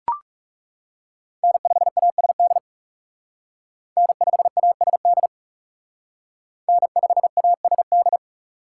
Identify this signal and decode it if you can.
Morse code, heard as a keyed tone, about 37 wpm, 700 Hz. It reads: N5ASD N5ASD N5ASD
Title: Speed Racing - USA 1x3 Call Signs 30wpm